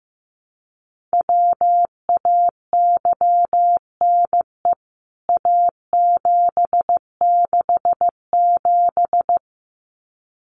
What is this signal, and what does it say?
Morse code, heard as a keyed tone, about 15 wpm, 700 Hz. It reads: WAYNE A767
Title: CWT Exchange 15wpm